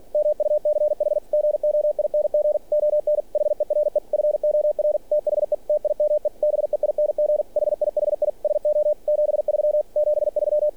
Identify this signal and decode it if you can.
Morse code, heard as a keyed tone, about 46 wpm, 600 Hz. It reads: MUCH GOING ON HERE FOR THE TIME BEING HIHI SO 73 73
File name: archived / CRSnet